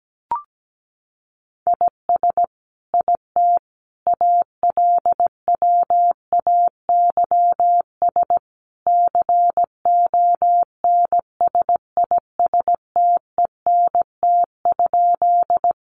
Morse code, heard as a keyed tone, about 17 wpm, 700 Hz. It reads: IS IT ALWAYS CONSISTENT?
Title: Sentences from Top 1500 Words 17wpm